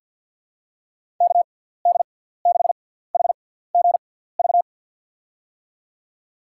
Morse code, heard as a keyed tone, about 51 wpm, spaced as for 17 wpm, 700 Hz. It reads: KD6HGV
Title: USA Call Signs - 6x Character Spacing - 50wpm